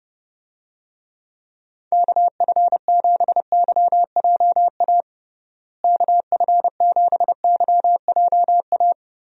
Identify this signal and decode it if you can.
Morse code, heard as a keyed tone, about 30 wpm, 700 Hz. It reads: KF7YJA KF7YJA
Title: USA Call Signs - 6 Characters - Repeated 3x - 30wpm